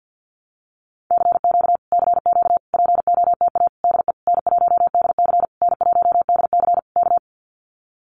Morse code, heard as a keyed tone, about 50 wpm, 700 Hz. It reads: CQ CQ POTA DE N1DC N1DC K
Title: Common QSO Phrases Encore 50wpm